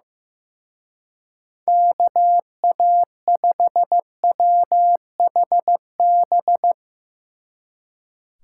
Morse code, heard as a keyed tone, about 16 wpm, 700 Hz. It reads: KA5WHB
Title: Speed Racing - USA 2x3 Call Signs 15wpm